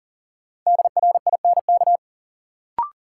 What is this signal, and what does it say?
Morse code, heard as a keyed tone, about 40 wpm, 700 Hz. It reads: DRINK
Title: Sentences from Top 1500 Words - New Word Drill Alt - 40wpm